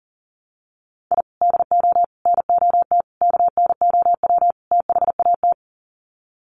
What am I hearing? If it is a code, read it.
Morse code, heard as a keyed tone, about 40 wpm, 700 Hz. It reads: I DO NOT KNOW THAT
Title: Sets of 5 Words 40wpm